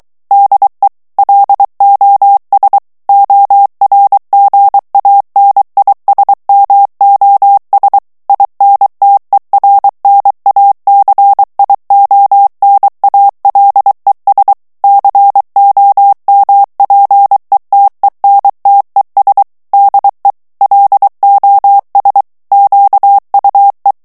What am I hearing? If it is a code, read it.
Morse code, heard as a keyed tone, about 23 wpm, 800 Hz. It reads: DE LOS ORGANISMOS INTERNACIONALES COMPETENTES DE LOS QUE
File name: morse mp3 / ESPAGNOL TEXTE EN CLAIR 2 VITESSE 22 MOTS MINUTES